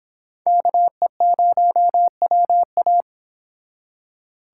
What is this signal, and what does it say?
Morse code, heard as a keyed tone, about 30 wpm, 700 Hz. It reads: KE0WA